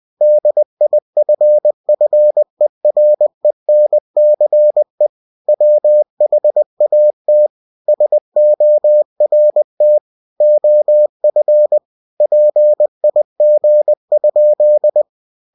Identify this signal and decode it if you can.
Morse code, heard as a keyed tone, about 20 wpm, 600 Hz. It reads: DIFFERENCE WHAT SORT OF PIG?
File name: cw-books / winnie-the-pooh / winnie-20wpm-page-0041